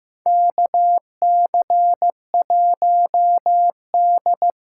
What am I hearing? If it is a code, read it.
Morse code, heard as a keyed tone, about 15 wpm, 700 Hz. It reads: KC1D
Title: USA Call Signs - 6 Characters - 6x Character Spacing - 15wpm